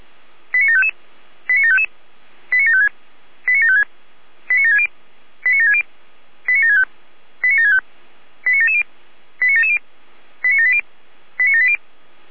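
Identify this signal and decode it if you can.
Five-tone selective call (ZVEI), heard as quick runs of five tones, tones of about 70 ms. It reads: 89755 89755 89756 89756 89760 89760 89765 89765 89770 89770 89780 89780
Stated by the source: ZVEI five-tone selective call received by SDR.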